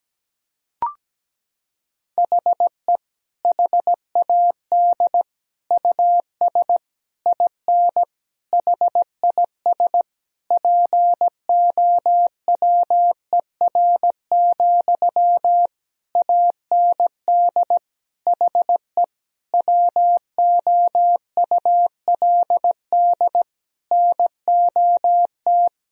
Morse code, heard as a keyed tone, about 17 wpm, 700 Hz. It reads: HE HAD US IN HIS POWER, AND HE WOULD NOT
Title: Sentences from Top 1000 Words - Review Part 2 - 17wpm